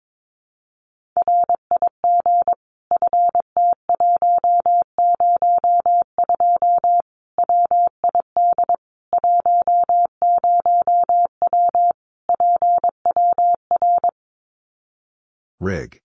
Morse code, heard as a keyed tone, about 22 wpm, 700 Hz. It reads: RIG FT102 WID 10W PWR
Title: Common QSO Phrases 22wpm